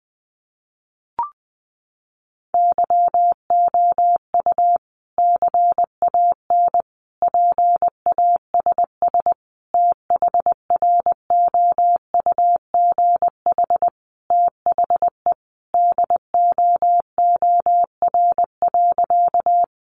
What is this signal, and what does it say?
Morse code, heard as a keyed tone, about 19 wpm, 700 Hz. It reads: YOU CAN PASS THROUGH THE DOOR.
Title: Sentences from Top 400 Words 20wpm